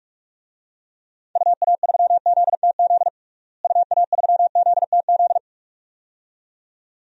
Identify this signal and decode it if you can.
Morse code, heard as a keyed tone, about 46 wpm, 700 Hz. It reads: UA3ZTZ UA3ZTZ
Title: DXCC-100 Call Signs - 6 Characters - Repeated 2x - 45wpm